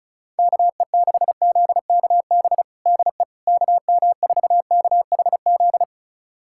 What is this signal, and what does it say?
Morse code, heard as a keyed tone, about 36 wpm, 700 Hz. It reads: KE6ZKB DE KM4KHZ
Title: USA Callsign DE Callsign 35wpm